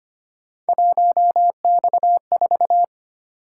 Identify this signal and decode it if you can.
Morse code, heard as a keyed tone, about 25 wpm, 700 Hz. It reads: 1X4
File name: Three Letter-Number Combinations - 25wpm